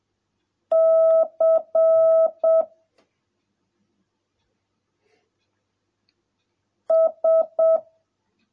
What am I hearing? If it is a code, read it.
Morse code, heard as a keyed tone, about 7 wpm, 650 Hz. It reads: C S